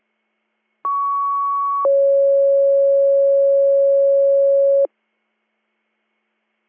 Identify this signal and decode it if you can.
Two-tone page, heard as a one-second tone, then a three-second tone, about 1120 then 550 Hz. It is heard from 0.8 to 4.9 s.